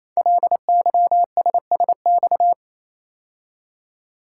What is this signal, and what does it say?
Morse code, heard as a keyed tone, about 27 wpm, 700 Hz. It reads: LYSSX
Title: Code Groups - 5 Letters - 28wpm